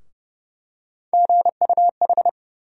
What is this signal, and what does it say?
Morse code, heard as a keyed tone, about 30 wpm, 700 Hz. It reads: GUH
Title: USA Call Sign Postfixes - 3 Letters 30wpm